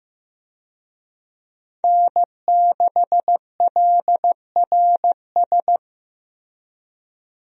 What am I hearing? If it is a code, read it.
Morse code, heard as a keyed tone, about 16 wpm, 700 Hz. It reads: N6LRS
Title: USA Call Signs - 5 Characters - Repeated 3x - 15wpm